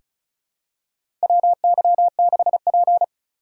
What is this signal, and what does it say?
Morse code, heard as a keyed tone, about 36 wpm, 700 Hz. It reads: WY6P
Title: USA Call Signs - 3x Character Spacing - 35wpm